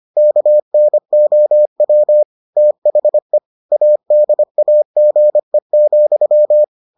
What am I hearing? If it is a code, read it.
Morse code, heard as a keyed tone, about 25 wpm, 600 Hz. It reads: KNOW THE ADAGE,